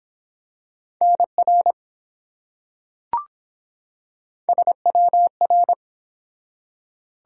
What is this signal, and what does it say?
Morse code, heard as a keyed tone, about 23 wpm, 700 Hz. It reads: NR SWR